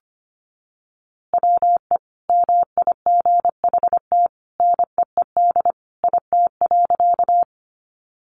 Morse code, heard as a keyed tone, about 25 wpm, 700 Hz. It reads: WE MIGHT NEED IT.
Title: A Sound Of Thunder 25wpm